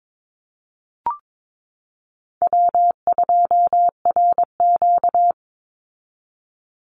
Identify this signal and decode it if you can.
Morse code, heard as a keyed tone, about 22 wpm, 700 Hz. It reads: W2RQ
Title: Call Signs Repeated 2x - 22wpm